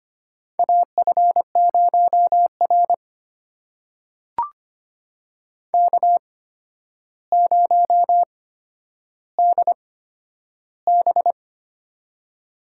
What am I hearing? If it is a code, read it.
Morse code, heard as a keyed tone, about 25 wpm, spaced as for 11 wpm, 700 Hz. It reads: AF0R K0DB
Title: USA Call Signs - 4 Characters - 8x Character Spacing - 25wpm